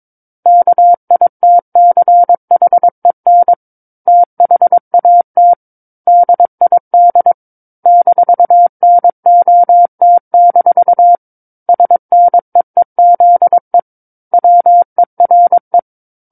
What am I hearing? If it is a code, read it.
Morse code, heard as a keyed tone, about 22 wpm, 700 Hz. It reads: KITCHEN THAT DID -NOT- SNEEZE WERE
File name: 06-Chapter-06 22 WPM